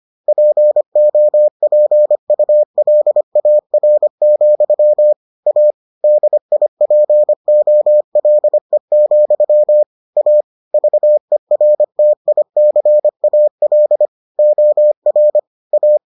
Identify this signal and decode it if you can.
Morse code, heard as a keyed tone, about 25 wpm, 600 Hz. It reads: POPULAR, A DIPOLE, A VERTICAL OR A